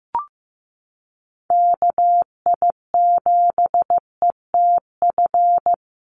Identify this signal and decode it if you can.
Morse code, heard as a keyed tone, about 15 wpm, 700 Hz. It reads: KI7ETF
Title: USA Call Signs Repeated 3x 15wpm